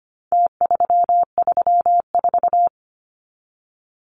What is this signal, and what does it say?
Morse code, heard as a keyed tone, about 25 wpm, 700 Hz. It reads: T334
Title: Numbers - Contest Style - 4-Digit 25wpm